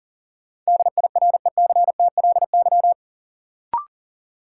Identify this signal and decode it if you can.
Morse code, heard as a keyed tone, about 40 wpm, 700 Hz. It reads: DIRECTLY
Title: Sentences from Top 800 Words - New Word Drill - 40wpm